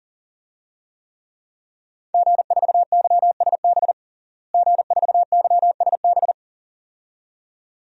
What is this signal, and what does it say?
Morse code, heard as a keyed tone, about 40 wpm, 700 Hz. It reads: G4YSB G4YSB
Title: DXCC-100 Call Signs - 5 Characters - Repeated 2x - 40wpm